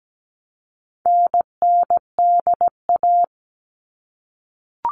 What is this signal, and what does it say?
Morse code, heard as a keyed tone, about 17 wpm, 700 Hz. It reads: NNDA
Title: All Cut Numbers - 4-Digit 17wpm